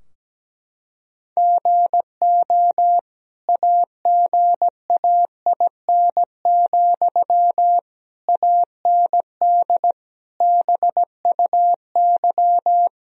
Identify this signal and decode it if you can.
Morse code, heard as a keyed tone, about 17 wpm, 700 Hz. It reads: GO AGAIN, AND BUY
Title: Sentences from Top 1000 Words - Review Part 4 - 17wpm